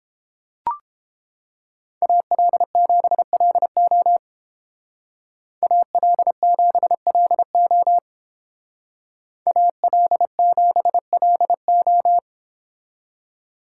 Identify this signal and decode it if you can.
Morse code, heard as a keyed tone, about 28 wpm, 700 Hz. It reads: AL7LO AL7LO AL7LO